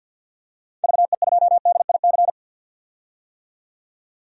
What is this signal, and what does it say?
Morse code, heard as a keyed tone, about 52 wpm, 700 Hz. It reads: VE2DIC
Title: DXCC-100 Call Signs - 6 Characters - 50wpm